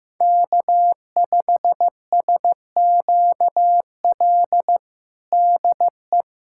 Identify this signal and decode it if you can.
Morse code, heard as a keyed tone, about 15 wpm, 700 Hz. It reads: K5SQL DE
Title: USA Callsign DE Callsign 15wpm